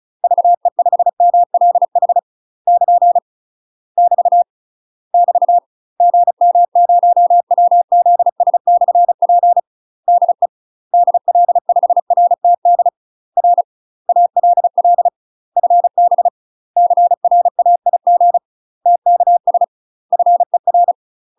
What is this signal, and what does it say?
Morse code, heard as a keyed tone, about 36 wpm, 700 Hz. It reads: VE5MLH <KN> <BT> <BT> GM0WZS/P DE DL5RTD R ALL FB CRAIG TKS FER
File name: UK MASTER QSOs TRAINING / 35WPM / UK MASTER QSOs 0035